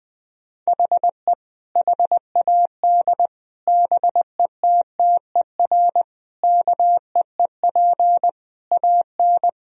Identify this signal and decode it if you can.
Morse code, heard as a keyed tone, about 20 wpm, 700 Hz. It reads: HE HAD BETTER KEEP AN
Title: Sentences from Top 1000 Words - Review Part 1 - 20wpm